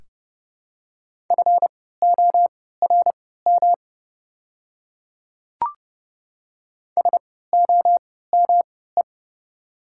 Morse code, heard as a keyed tone, about 30 wpm, spaced as for 17 wpm, 700 Hz. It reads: FORM SOME
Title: 3x Character Spacing - Top 300 Words - 30wpm